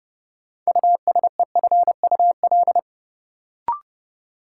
Morse code, heard as a keyed tone, about 29 wpm, 700 Hz. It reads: USEFUL
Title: Sentences from Top 700 Words - New Word Drill - 30wpm